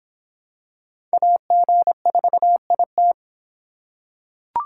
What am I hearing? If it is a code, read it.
Morse code, heard as a keyed tone, about 26 wpm, 700 Hz. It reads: AG4IT